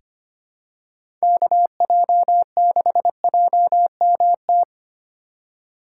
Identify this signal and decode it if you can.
Morse code, heard as a keyed tone, about 25 wpm, 700 Hz. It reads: KJ6JMT
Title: USA Call Signs - 6 Characters - 3x Character Spacing - 25wpm